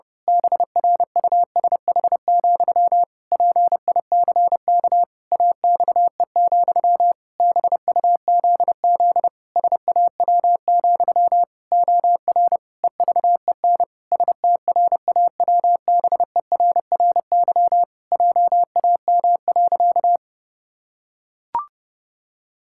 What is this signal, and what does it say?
Morse code, heard as a keyed tone, about 30 wpm, 700 Hz. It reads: BRUSH, PICK AXE, BUZZ SAW, OR EVEN STRAWBERRY JAM.
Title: Pangram Challenge v2 - 30wpm